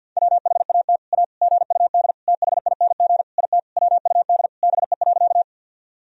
Morse code, heard as a keyed tone, about 50 wpm, 700 Hz. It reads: WHAT A GUD THING IT WUD BE.
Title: Sentences from Top 200 Words and Abbreviations 50wpm